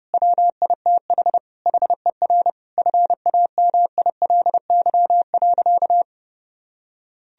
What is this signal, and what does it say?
Morse code, heard as a keyed tone, about 30 wpm, 700 Hz. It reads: WITH HER FAMILY.
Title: Sentences from Top 1000 Words - Review Part 4 - 30wpm